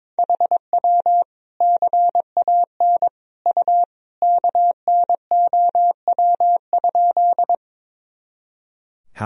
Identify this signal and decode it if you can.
Morse code, heard as a keyed tone, about 22 wpm, 700 Hz. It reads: HW CAN U KNOW?
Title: Sentences from Top 100 Words and Abbreviations 22wpm